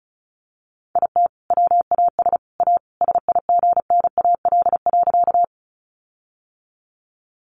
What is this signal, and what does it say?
Morse code, heard as a keyed tone, about 34 wpm, 700 Hz. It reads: IT WAS A SIGNAL.
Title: Sentences from Top 1400 Words 35wpm